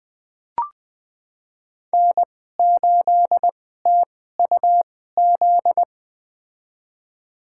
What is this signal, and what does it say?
Morse code, heard as a keyed tone, about 20 wpm, spaced as for 14 wpm, 700 Hz. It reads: N8TUZ
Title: USA Call Signs - 5 Characters - 2x Character Spacing - 20wpm